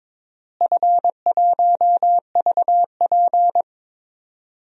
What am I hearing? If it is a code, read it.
Morse code, heard as a keyed tone, about 21 wpm, 700 Hz. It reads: F1VP
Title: DXCC-100 Call Signs - 4 Characters - Repeated 3x - 22wpm